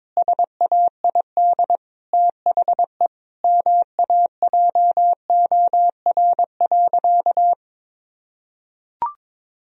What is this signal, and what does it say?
Morse code, heard as a keyed tone, about 22 wpm, 700 Hz. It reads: SAID THE MAJOR.
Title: Sentences from Top 1000 Words - Review Part 4 - 22wpm